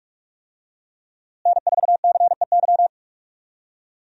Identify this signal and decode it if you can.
Morse code, heard as a keyed tone, about 46 wpm, 700 Hz. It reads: N4CEY
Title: USA Call Signs - 5 Characters - Repeated 2x - 45wpm